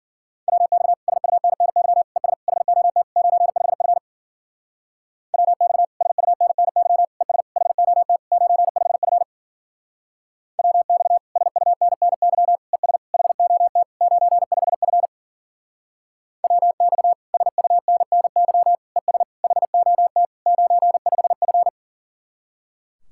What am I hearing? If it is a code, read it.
Morse code, heard as a keyed tone, about 48 wpm, 700 Hz. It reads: WX SUNNY ES HOT 95F WX SUNNY ES HOT 95F WX SUNNY ES HOT 95F WX SUNNY ES HOT 95F
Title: Speed-Racing Common QSO Phrases Encore - 40wpm